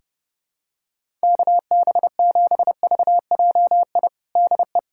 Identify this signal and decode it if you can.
Morse code, heard as a keyed tone, about 30 wpm, 700 Hz. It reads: KB7VJI DE
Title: USA Callsign DE Callsign - 6 Characters - 30wpm